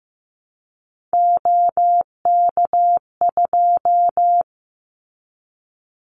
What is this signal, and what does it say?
Morse code, heard as a keyed tone, about 15 wpm, 700 Hz. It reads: OK2
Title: Callsign Prefixes for Top 50 DXCC - 15wpm